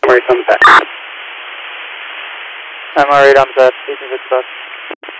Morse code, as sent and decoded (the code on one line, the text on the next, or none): none